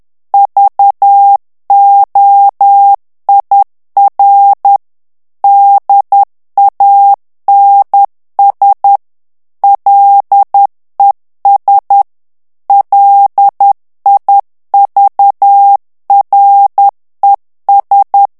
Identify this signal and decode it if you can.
Morse code, heard as a keyed tone, about 10 wpm, 800 Hz. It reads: VOIR DANS LES LIVRES
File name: morse mp3 / FRANCAIS TEXTE CLAIR 2 VITESSE 10 MOTS MINUTES